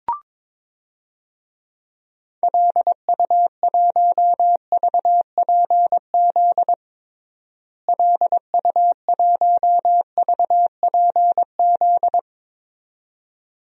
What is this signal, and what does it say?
Morse code, heard as a keyed tone, about 22 wpm, 700 Hz. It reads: LU1VPZ LU1VPZ
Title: DXCC-100 Call Signs - 6 Characters - Repeated 2x - 22wpm